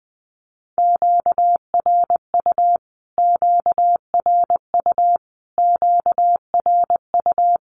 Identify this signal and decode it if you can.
Morse code, heard as a keyed tone, about 20 wpm, 700 Hz. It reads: QRU QRU QRU
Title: Q-Codes Repeated 3x 20wpm